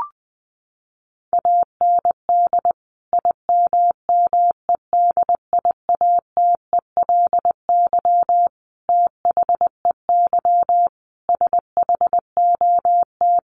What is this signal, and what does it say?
Morse code, heard as a keyed tone, about 20 wpm, 700 Hz. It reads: AND IMMEDIATELY THEY SHOT
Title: Sentences from Top 1100 Words 20wpm